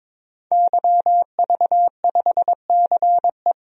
Morse code, heard as a keyed tone, about 22 wpm, 700 Hz. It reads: YV5CE
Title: DXCC-100 Call Signs - 5 Characters - 22wpm